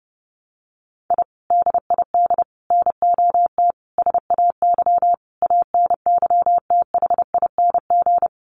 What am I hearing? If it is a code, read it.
Morse code, heard as a keyed tone, about 30 wpm, 700 Hz. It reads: I DID NOT SAY ANYTHING
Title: Sets of 5 Words Encore2 30wpm